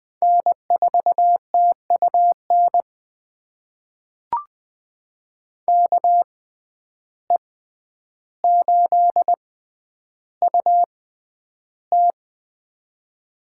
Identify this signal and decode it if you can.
Morse code, heard as a keyed tone, about 21 wpm, spaced as for 10 wpm, 700 Hz. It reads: N4TUN KE8UT
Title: USA Call Signs - 5 Characters - 6x Character Spacing - 20wpm